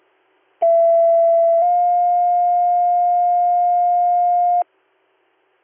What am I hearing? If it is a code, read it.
Two-tone page, heard as a one-second tone, then a three-second tone, about 670 then 710 Hz. It is heard from 0.6 to 4.6 s.